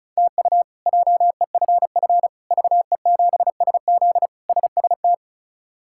Morse code, heard as a keyed tone, about 34 wpm, 700 Hz. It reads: TU JEFF VE7SZ SST
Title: K1USN SST QSO Practice - 35wpm